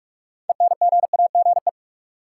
Morse code, heard as a keyed tone, about 46 wpm, 700 Hz. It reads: ENGAGE
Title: Sentences from Top 1500 Words - New Word Drill - 45wpm